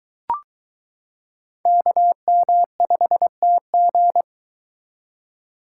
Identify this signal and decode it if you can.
Morse code, heard as a keyed tone, about 23 wpm, 700 Hz. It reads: KM5TG